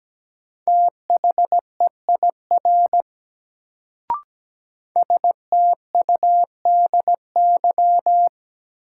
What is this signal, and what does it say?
Morse code, heard as a keyed tone, about 17 wpm, 700 Hz. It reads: THEIR STUDY
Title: Sentences from Top 300 Words - New Word Drill - 17wpm